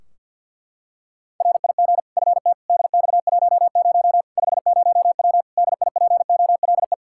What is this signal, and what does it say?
Morse code, heard as a keyed tone, about 52 wpm, 700 Hz. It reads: RIG FT DX10 50W DIPOLE
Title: K1USN SST QSO Practice 50wpm